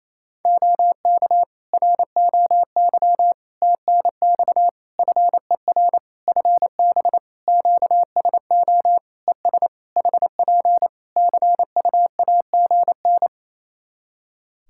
Morse code, heard as a keyed tone, about 27 wpm, 700 Hz. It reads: OK ROY TNX FER FB QSO ES HP CUAGN
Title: Practice QSO - Exchange 4 - Ending - 28wpm